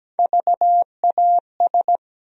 Morse code, heard as a keyed tone, about 17 wpm, 700 Hz. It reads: VAS
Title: USA Call Sign Postfixes - 3 Letters 17wpm